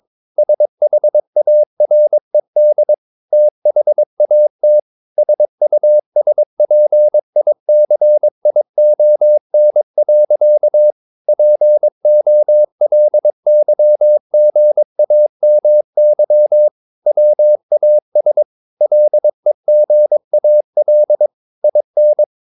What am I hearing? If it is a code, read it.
Morse code, heard as a keyed tone, about 22 wpm, 600 Hz. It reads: SHARED THAT SUSPICION. POLYGAMY WAS LEGAL IN
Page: Romanen Never af Ken Follet på engelsk.